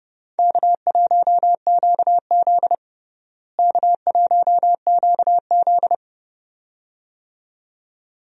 Morse code, heard as a keyed tone, about 32 wpm, 700 Hz. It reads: K1QZ K1QZ
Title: USA Call Signs - 4 Characters - Repeated 2x - 30wpm